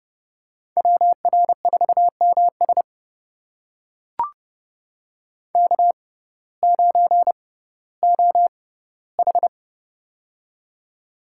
Morse code, heard as a keyed tone, about 30 wpm, spaced as for 17 wpm, 700 Hz. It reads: WR4MS K9OH
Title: USA Call Signs - 6x Character Spacing - 30wpm